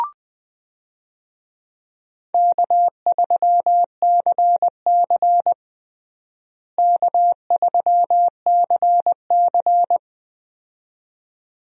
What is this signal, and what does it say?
Morse code, heard as a keyed tone, about 21 wpm, 700 Hz. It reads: K3CC K3CC
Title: USA Call Signs - 4 Characters - Repeated 2x - 20wpm